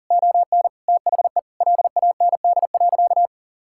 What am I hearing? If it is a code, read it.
Morse code, heard as a keyed tone, about 40 wpm, 700 Hz. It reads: ON THE LAND.
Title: Sentences from Top 1000 Words - Review Part 5 - 40wpm